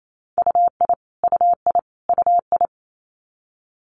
Morse code, heard as a keyed tone, about 28 wpm, 700 Hz. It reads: UI UI UI
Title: Multi-letter Phonemes Repeated 3x 28wpm